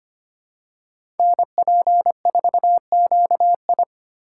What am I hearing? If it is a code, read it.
Morse code, heard as a keyed tone, about 25 wpm, 700 Hz. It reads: NP4QI
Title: USA Call Signs - 5 Characters - 3x Character Spacing - 25wpm